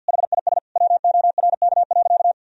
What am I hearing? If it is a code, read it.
Morse code, heard as a keyed tone, about 52 wpm, 700 Hz. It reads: HIS WORK.
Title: Sentences from Top 1000 Words - Review Part 4 - 50wpm